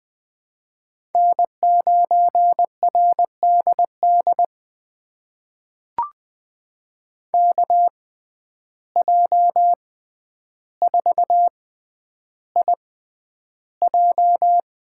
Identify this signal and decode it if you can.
Morse code, heard as a keyed tone, about 20 wpm, spaced as for 10 wpm, 700 Hz. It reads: N9RDD KJ4IJ
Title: USA Call Signs - 5 Characters - 6x Character Spacing - 20wpm